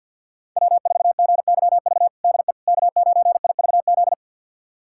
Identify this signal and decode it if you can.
Morse code, heard as a keyed tone, about 46 wpm, 700 Hz. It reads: W4GYV DE K9IVZ
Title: USA Callsign DE Callsign - 5 Characters - 50wpm